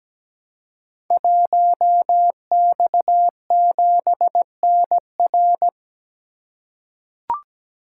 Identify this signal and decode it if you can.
Morse code, heard as a keyed tone, about 17 wpm, 700 Hz. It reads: 1X7NR
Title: Code Groups - 5 Letter:Numbers - 17wpm